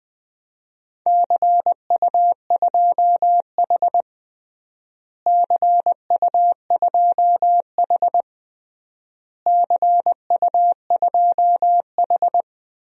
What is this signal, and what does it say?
Morse code, heard as a keyed tone, about 20 wpm, 700 Hz. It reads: CU2H CU2H CU2H
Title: DXCC-100 Call Signs - 4 Characters - Repeated 3x - 20wpm